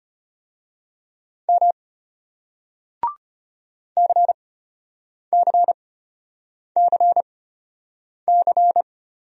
Morse code, heard as a keyed tone, about 35 wpm, spaced as for 8 wpm, 700 Hz. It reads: M CCCC